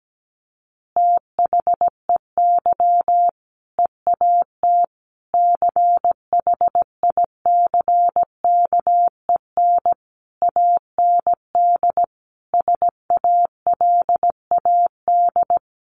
Morse code, heard as a keyed tone, about 17 wpm, 700 Hz. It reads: THEY EAT CHICKEN AND SALAD
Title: Easy Phrases 17wpm